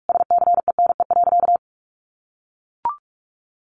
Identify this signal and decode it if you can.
Morse code, heard as a keyed tone, about 44 wpm, 700 Hz. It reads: SCENE.
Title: Sentences from Top 1100 Words 45wpm